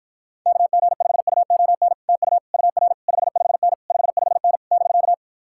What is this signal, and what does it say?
Morse code, heard as a keyed tone, about 52 wpm, 700 Hz. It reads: KG5UON TU UR 55N 55N <BK>